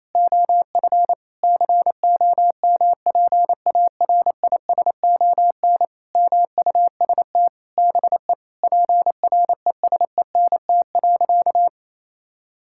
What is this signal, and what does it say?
Morse code, heard as a keyed tone, about 28 wpm, 700 Hz. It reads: OF COMPARISON MUST BE PRESENT.
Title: Sentences from Top 1600 Words 28wpm